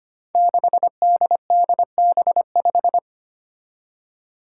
Morse code, heard as a keyed tone, about 25 wpm, 700 Hz. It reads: 6DDB5
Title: Five Challenging Letter-Number Combinations 25wpm